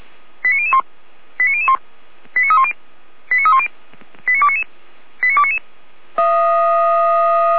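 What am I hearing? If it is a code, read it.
Five-tone selective call (ZVEI), heard as quick runs of five tones, tones of about 70 ms. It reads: 89001 89001 89310 89310 89299 89299
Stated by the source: ZVEI five-tone selective call 89303 received on SDR.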